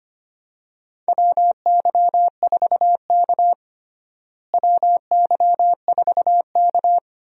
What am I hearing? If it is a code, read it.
Morse code, heard as a keyed tone, about 25 wpm, 700 Hz. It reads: WY4K WY4K
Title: USA Call Signs - 4 Characters - Repeated 2x - 25wpm